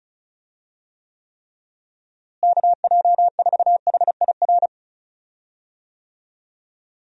Morse code, heard as a keyed tone, about 34 wpm, 700 Hz. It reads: KJ4HIR
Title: USA Call Signs - 6 Characters - Repeated 3x - 35wpm